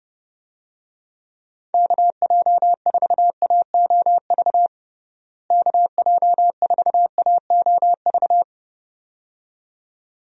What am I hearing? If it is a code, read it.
Morse code, heard as a keyed tone, about 30 wpm, 700 Hz. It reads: KJ4AOV KJ4AOV
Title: USA Call Signs - 6 Characters - Repeated 2x - 30wpm